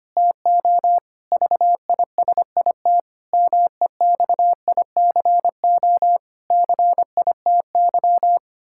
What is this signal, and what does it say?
Morse code, heard as a keyed tone, about 25 wpm, 700 Hz. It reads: TO VISIT MEXICO CITY
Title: Pangram Challenge v2 - 25wpm